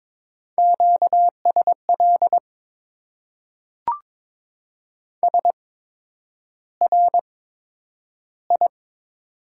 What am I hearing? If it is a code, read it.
Morse code, heard as a keyed tone, about 21 wpm, spaced as for 9 wpm, 700 Hz. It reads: QSL SRI
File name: Top 90 QSO Elements - 8x Character Spacing - 22wpm